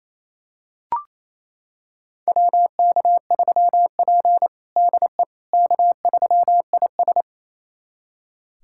Morse code, heard as a keyed tone, about 28 wpm, 700 Hz. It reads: WK3P DE K3IS
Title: USA Callsign DE Callsign - 4 Characters - 28wpm